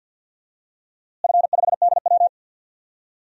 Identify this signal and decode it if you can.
Morse code, heard as a keyed tone, about 46 wpm, 700 Hz. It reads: F5DW
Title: DXCC-100 Call Signs - 4 Characters - Repeated 3x - 50wpm